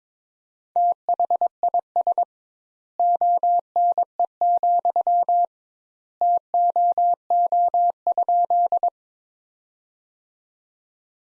Morse code, heard as a keyed tone, about 22 wpm, 700 Hz. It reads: THIS ONE, TOO?
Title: Sentences Top 200 Words - 22wpm